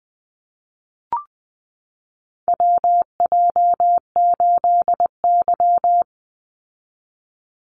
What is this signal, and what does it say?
Morse code, heard as a keyed tone, about 20 wpm, 700 Hz. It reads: WJ8Y
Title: Call Signs Repeated 2x - 20wpm